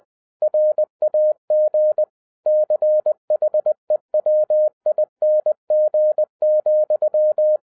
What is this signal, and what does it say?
Morse code, heard as a keyed tone, about 20 wpm, 610 Hz. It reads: RAG CHEWING,
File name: 2011-10-mp3 / n13-TIPS FROM MY 34 YEARS OF EXPERIENCE 20wpm